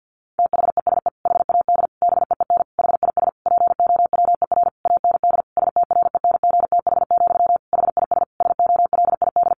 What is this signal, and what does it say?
Morse code, heard as a keyed tone, about 50 wpm, 700 Hz. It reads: THESE HAD BEEN HIS POWER AND STRENGTH, HIS SOLID
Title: Sentences from Top 1500 Words 50wpm